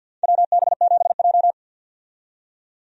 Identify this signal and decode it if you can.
Morse code, heard as a keyed tone, about 47 wpm, 700 Hz. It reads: WB7J
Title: USA Call Signs - 4 Characters - 50wpm